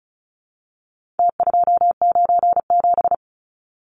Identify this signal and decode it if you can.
Morse code, heard as a keyed tone, about 34 wpm, 700 Hz. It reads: T297
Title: Numbers - Contest Style - 4-Digit 35wpm